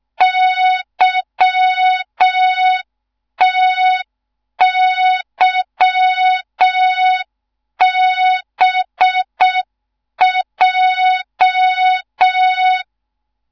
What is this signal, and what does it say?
Morse code, heard as a keyed tone, about 6 wpm, 3700 Hz. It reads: YTYBJ
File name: Morze-HA2MN / hang / 15 GYAKORLAT